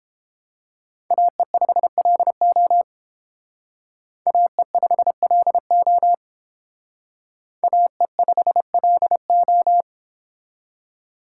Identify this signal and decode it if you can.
Morse code, heard as a keyed tone, about 30 wpm, 700 Hz. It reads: AE5LO AE5LO AE5LO